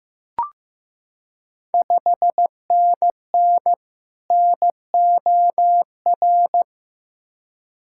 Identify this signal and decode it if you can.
Morse code, heard as a keyed tone, about 15 wpm, 690 Hz. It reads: 5NN NOR
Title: Ontario QSO Party Exchange 15wpm